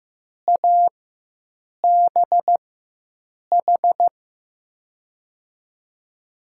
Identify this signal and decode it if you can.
Morse code, heard as a keyed tone, about 16 wpm, spaced as for 7 wpm, 700 Hz. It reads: ABH
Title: USA Call Signs - 6 Characters - 4x Character Spacing - 15wpm